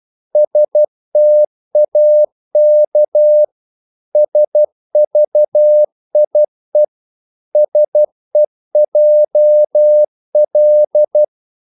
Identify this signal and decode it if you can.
Morse code, heard as a keyed tone, about 12 wpm, 600 Hz. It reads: STAK SVIE SEJL
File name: Korte ord DK 12wpm